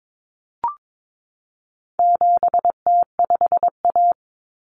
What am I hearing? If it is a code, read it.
Morse code, heard as a keyed tone, about 22 wpm, 700 Hz. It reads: 7T5A
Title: Common Cut Numbers - 4-Digit 22wpm